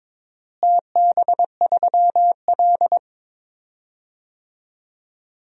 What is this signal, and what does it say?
Morse code, heard as a keyed tone, about 21 wpm, 700 Hz. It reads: TB3L
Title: DXCC-100 Call Signs - 4 Characters - Repeated 3x - 22wpm